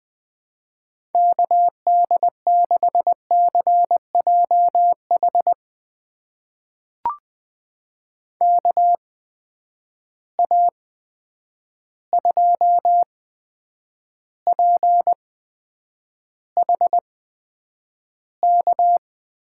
Morse code, heard as a keyed tone, about 20 wpm, spaced as for 9 wpm, 700 Hz. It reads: KD6CJH KA2PHK
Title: USA Call Signs - 6 Characters - 8x Character Spacing - 20wpm